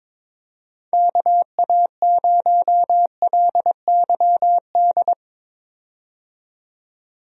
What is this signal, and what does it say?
Morse code, heard as a keyed tone, about 22 wpm, 700 Hz. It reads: KA0LYD
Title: USA Call Signs - 6 Characters - 22wpm